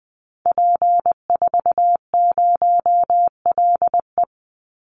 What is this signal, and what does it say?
Morse code, heard as a keyed tone, about 20 wpm, 700 Hz. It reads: P40LE
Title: Call Signs Repeated 3x - 20wpm